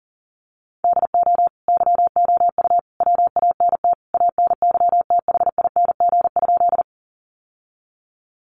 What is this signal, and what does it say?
Morse code, heard as a keyed tone, about 40 wpm, 700 Hz. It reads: DO YOU WANT ANYTHING?
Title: Sentences from Top 1000 Words 40wpm